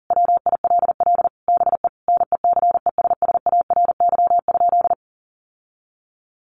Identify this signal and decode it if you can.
Morse code, heard as a keyed tone, about 40 wpm, 700 Hz. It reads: WILL BE NECESSARY?
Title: Sentences from Top 800 Words 40wpm